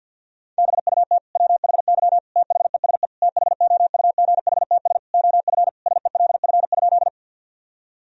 Morse code, heard as a keyed tone, about 51 wpm, 700 Hz. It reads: BUT WHY THESE THOUGHTS OF SELF?
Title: Sentences from Top 1000 Words - Review Part 1 - 50wpm